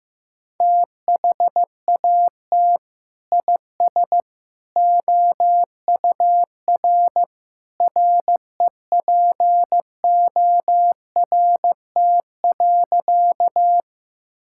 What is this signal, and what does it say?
Morse code, heard as a keyed tone, about 15 wpm, 700 Hz. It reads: THAT IS OUR REPORT.
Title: Sentences from Top 1000 Words - Review Part 6 - 15wpm